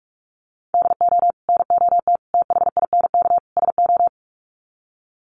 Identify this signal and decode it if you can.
Morse code, heard as a keyed tone, about 44 wpm, 700 Hz. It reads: DO NOT THINK SO
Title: Sets of 4 Words 45wpm